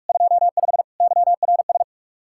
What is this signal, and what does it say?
Morse code, heard as a keyed tone, about 59 wpm, spaced as for 52 wpm, 700 Hz. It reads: 25 YRS
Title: Speed-Racing Common QSO Phrases Encore - 45wpm